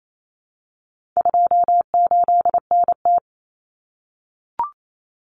Morse code, heard as a keyed tone, about 28 wpm, 700 Hz. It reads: 28NT
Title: Common Cut Numbers - 4-Digit 28wpm